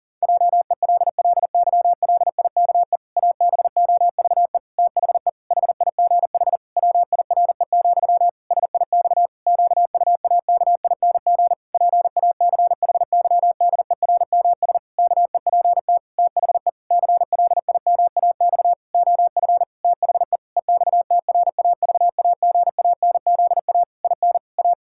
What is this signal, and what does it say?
Morse code, heard as a keyed tone, about 40 wpm, 700 Hz. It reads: JELLYLIKE ABOVE THE HIGH WIRE, SIX QUAKING PACHYDERMS KEPT THE CLIMAX OF THE EXTRAVAGANZA IN A
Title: Pangram Challenge v2 - 40wpm